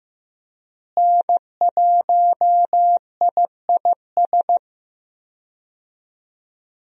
Morse code, heard as a keyed tone, about 15 wpm, 700 Hz. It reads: N1IIS
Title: USA Call Signs - 5 Characters - Repeated 3x - 15wpm